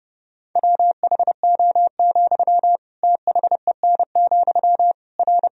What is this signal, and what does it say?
Morse code, heard as a keyed tone, about 32 wpm, 700 Hz. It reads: WHO, THEN, R
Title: Sentences from Top 100 Words and Abbreviations 30wpm